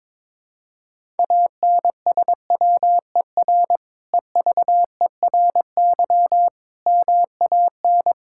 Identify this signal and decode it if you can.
Morse code, heard as a keyed tone, about 22 wpm, 700 Hz. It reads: ANSWER EVERY MAN
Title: Sentences from Top 1000 Words - Review Part 5 - 22wpm